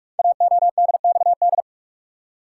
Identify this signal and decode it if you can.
Morse code, heard as a keyed tone, about 44 wpm, 700 Hz. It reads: AODXD
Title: Code Groups - 5 Letters - 45wpm